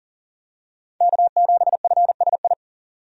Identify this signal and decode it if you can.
Morse code, heard as a keyed tone, about 40 wpm, 700 Hz. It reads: K7FSI
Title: USA Call Signs - 5 Characters - 40wpm